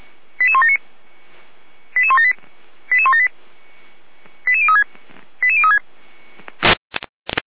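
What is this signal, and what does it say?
Five-tone selective call (ZVEI), heard as quick runs of five tones, tones of about 70 ms. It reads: 88179 88178 88178 88036 88036
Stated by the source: Received on an SDR.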